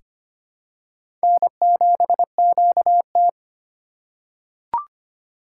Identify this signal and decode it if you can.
Morse code, heard as a keyed tone, about 25 wpm, 700 Hz. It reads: N7QT
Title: K1USN SST QSO Practice - 25wpm